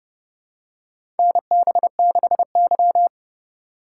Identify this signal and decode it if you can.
Morse code, heard as a keyed tone, about 32 wpm, 700 Hz. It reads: NB6Y
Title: USA Call Signs - 4 Characters - 2x Character Spacing - 30wpm